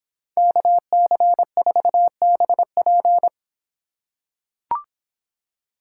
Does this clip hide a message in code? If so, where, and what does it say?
Yes: Morse code from 0.4 to 3.3 s, heard as a keyed tone, about 26 wpm, 700 Hz. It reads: KC4BP